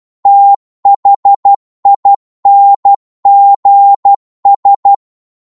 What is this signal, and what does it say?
Morse code, heard as a keyed tone, about 12 wpm, 800 Hz. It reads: THINGS
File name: 12wpm